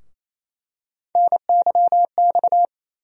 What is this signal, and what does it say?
Morse code, heard as a keyed tone, about 28 wpm, 700 Hz. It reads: NYX
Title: USA Call Sign Postfixes - 3 Letters 28wpm